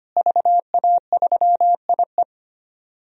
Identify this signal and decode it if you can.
Morse code, heard as a keyed tone, about 25 wpm, 700 Hz. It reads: VA3IE
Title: DXCC-100 Call Signs - 5 Characters - Repeated 2x - 25wpm